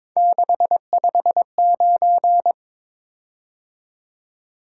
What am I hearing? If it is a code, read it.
Morse code, heard as a keyed tone, about 22 wpm, 700 Hz. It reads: 659
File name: practice / 22wpm / Three Letter-Number Combinations - 22wpm